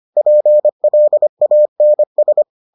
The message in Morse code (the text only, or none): PLANS